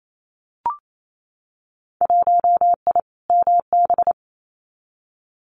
Morse code, heard as a keyed tone, about 28 wpm, 700 Hz. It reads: 1I MB
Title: Winter Field Day Exchanges Encore 28wpm